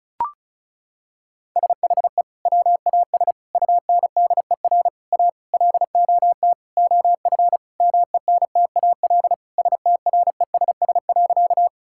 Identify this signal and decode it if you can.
Morse code, heard as a keyed tone, about 34 wpm, 700 Hz. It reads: SHE WAS UNDER A LOT OF MENTAL STRESS.
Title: Sentences from Top 1000 Words - Review Part 6 - 35wpm